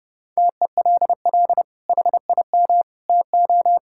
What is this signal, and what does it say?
Morse code, heard as a keyed tone, about 30 wpm, 700 Hz. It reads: TELL HIM TO
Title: Sentences from Top 1000 Words - Review Part 6 - 30wpm